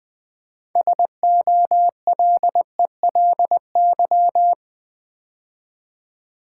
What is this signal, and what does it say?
Morse code, heard as a keyed tone, about 20 wpm, 700 Hz. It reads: SOLELY
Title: Build A Word - 3 Letters At A Time - 6-Letter Words - 20wpm